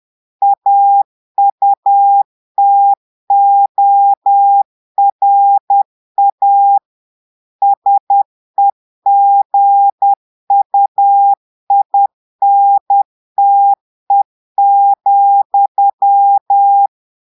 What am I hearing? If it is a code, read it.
Morse code, heard as a keyed tone, about 10 wpm, 800 Hz. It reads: AUTORA SEGUINTE,
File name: Portugais - texte en clair 1 - vitesse - 10 mots minute